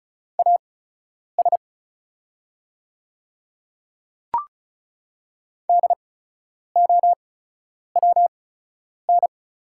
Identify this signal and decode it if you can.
Morse code, heard as a keyed tone, about 36 wpm, spaced as for 10 wpm, 700 Hz. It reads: AS DOWN
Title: 8x Character Spacing - Top 100 Words - 35wpm